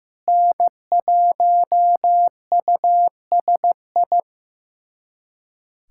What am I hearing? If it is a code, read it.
Morse code, heard as a keyed tone, about 15 wpm, 700 Hz. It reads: N1USI
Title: USA Callsign DE Callsign - 5 Characters - 15wpm